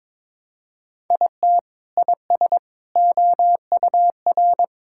Morse code, heard as a keyed tone, about 22 wpm, 700 Hz. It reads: IT IS OUR
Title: Sentences from Top 1000 Words - Review Part 6 - 22wpm